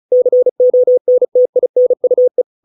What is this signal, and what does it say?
Morse code, heard as a keyed tone, about 35 wpm, 500 Hz. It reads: CONTINUE